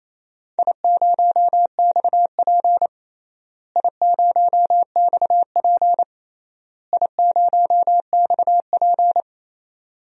Morse code, heard as a keyed tone, about 28 wpm, 700 Hz. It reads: I0XP I0XP I0XP
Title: DXCC-100 Call Signs - 4 Characters - Repeated 3x - 28wpm